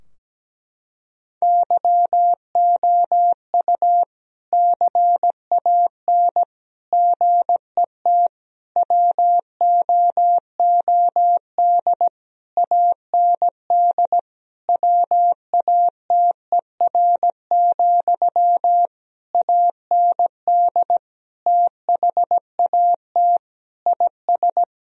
Morse code, heard as a keyed tone, about 17 wpm, 700 Hz. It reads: YOU CAN GET WOOD AND WATER, AND THAT IS
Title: Sentences from Top 1000 Words - Review Part 2 - 17wpm